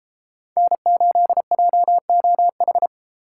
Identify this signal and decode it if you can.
Morse code, heard as a keyed tone, about 30 wpm, 700 Hz. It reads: N8JOH